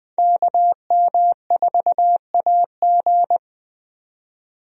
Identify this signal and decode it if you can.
Morse code, heard as a keyed tone, about 20 wpm, 700 Hz. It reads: KM4AG
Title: USA Call Signs - 5 Characters - 20wpm